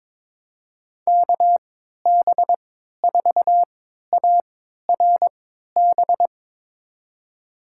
Morse code, heard as a keyed tone, about 22 wpm, spaced as for 13 wpm, 700 Hz. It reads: KB4ARB
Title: USA Call Signs - 6 Characters - 3x Character Spacing - 22wpm